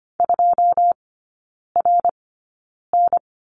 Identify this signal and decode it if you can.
Morse code, heard as a keyed tone, about 25 wpm, spaced as for 9 wpm, 690 Hz. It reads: 2RN
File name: Lesson 15e - Number 5 - Three Characters 25wpm